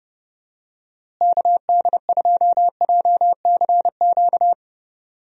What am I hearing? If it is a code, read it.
Morse code, heard as a keyed tone, about 30 wpm, 700 Hz. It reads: KD2JCQ
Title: USA Call Signs - 6 Characters - Repeated 3x - 30wpm